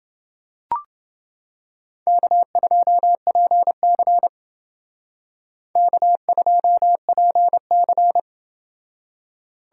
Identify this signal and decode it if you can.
Morse code, heard as a keyed tone, about 28 wpm, 700 Hz. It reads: K2PC K2PC